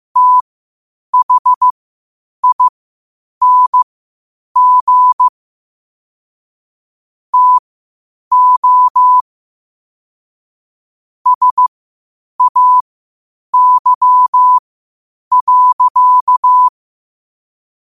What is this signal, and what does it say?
Morse code, heard as a keyed tone, about 15 wpm, spaced as for 8 wpm, 1000 Hz. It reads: THING TO SAY.
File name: qotdfiles / 8wpm